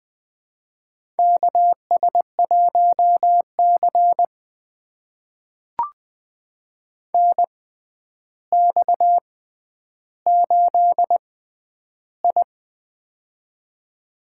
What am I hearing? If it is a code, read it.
Morse code, heard as a keyed tone, about 20 wpm, spaced as for 10 wpm, 700 Hz. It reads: KS1C NX8I
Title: USA Call Signs - 4 Characters - 6x Character Spacing - 20wpm